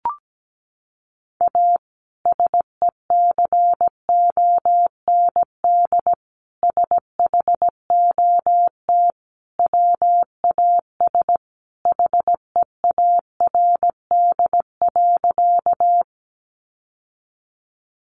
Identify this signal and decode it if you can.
Morse code, heard as a keyed tone, about 17 wpm, 700 Hz. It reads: A SECOND SHOT WAS HEARD.
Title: Sentences from Top 1100 Words 17wpm